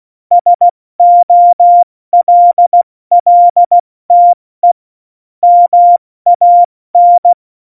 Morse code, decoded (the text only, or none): SOLLTE MAN